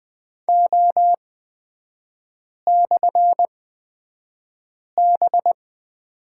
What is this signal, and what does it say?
Morse code, heard as a keyed tone, about 20 wpm, spaced as for 5 wpm, 700 Hz. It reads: O/B
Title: 250 Lesson 40e - BK - Three Characters 20wpm